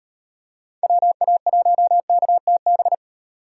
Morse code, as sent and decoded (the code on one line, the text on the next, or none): .-- .- .---- -.- - -...
WA1KTB